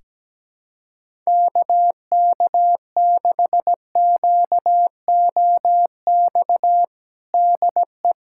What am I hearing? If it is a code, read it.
Morse code, heard as a keyed tone, about 17 wpm, 700 Hz. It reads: KK6QOX DE
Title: USA Callsign DE Callsign - 6 Characters - 17wpm